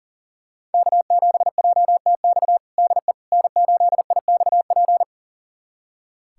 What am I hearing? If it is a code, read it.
Morse code, heard as a keyed tone, about 40 wpm, 700 Hz. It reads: K7JTX DE N8IXP
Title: USA Callsign DE Callsign - 5 Characters - 40wpm